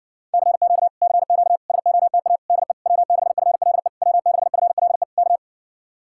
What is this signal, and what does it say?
Morse code, heard as a keyed tone, about 69 wpm, 700 Hz. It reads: CQ CQ SOTA DE W6FLE W6FLE K